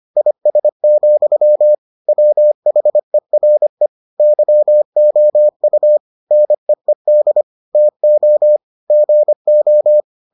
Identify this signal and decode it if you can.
Morse code, heard as a keyed tone, about 25 wpm, 600 Hz. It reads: IS, WHERE YOU NEED TO GO